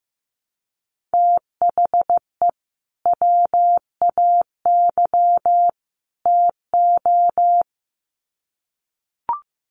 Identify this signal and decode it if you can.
Morse code, heard as a keyed tone, about 15 wpm, 700 Hz. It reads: THE WAY TO
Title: Sets of 3 Words 15wpm